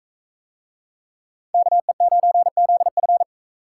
Morse code, heard as a keyed tone, about 41 wpm, 700 Hz. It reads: KE9ZF